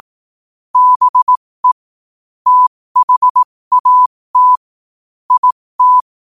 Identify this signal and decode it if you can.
Morse code, heard as a keyed tone, about 18 wpm, spaced as for 15 wpm, 1000 Hz. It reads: BE THAT IT
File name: qotdfiles / 15wpm